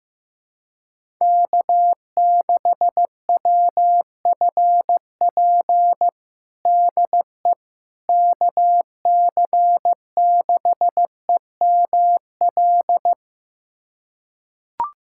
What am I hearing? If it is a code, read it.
Morse code, heard as a keyed tone, about 15 wpm, 700 Hz. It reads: K6WFP DE KC6EML
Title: USA Callsign DE Callsign 15wpm